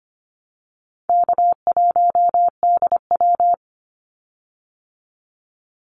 Morse code, heard as a keyed tone, about 25 wpm, 700 Hz. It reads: K1DW
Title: Easy Call Signs - 25wpm